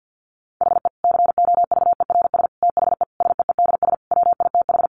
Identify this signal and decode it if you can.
Morse code, heard as a keyed tone, about 50 wpm, 700 Hz. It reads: HE COVERS THE SEEDS WITH
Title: Easy Phrases 50wpm